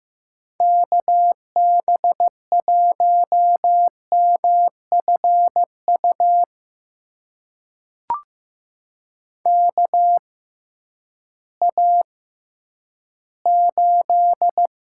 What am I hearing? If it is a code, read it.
Morse code, heard as a keyed tone, about 15 wpm, spaced as for 10 wpm, 700 Hz. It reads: KB1MFU KA8
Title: USA Call Signs - 6 Characters - 6x Character Spacing - 15wpm